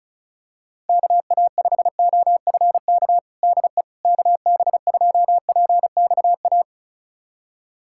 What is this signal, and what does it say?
Morse code, heard as a keyed tone, about 36 wpm, 700 Hz. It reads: KA5OFK DE KB2PXA
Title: USA Callsign DE Callsign - 6 Characters - 35wpm